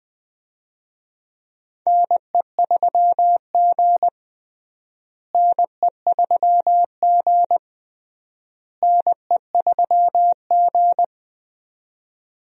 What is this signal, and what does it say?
Morse code, heard as a keyed tone, about 21 wpm, 700 Hz. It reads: NE3G NE3G NE3G
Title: USA Call Signs - 4 Characters - Repeated 3x - 20wpm